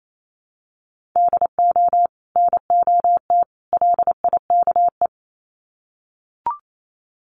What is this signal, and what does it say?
Morse code, heard as a keyed tone, about 28 wpm, 700 Hz. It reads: DO NOT LIKE
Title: Sets of 3 Words 28wpm